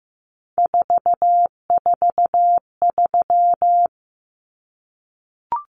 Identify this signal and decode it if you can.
Morse code, heard as a keyed tone, about 15 wpm, 700 Hz. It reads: 443
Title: Common Cut Numbers - 3-Digit 15wpm